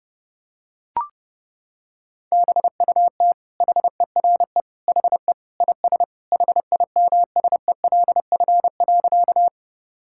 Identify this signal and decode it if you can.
Morse code, heard as a keyed tone, about 30 wpm, 700 Hz. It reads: BUT HERE HE IS HIMSELF.
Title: Sentences from Top 1600 Words 30wpm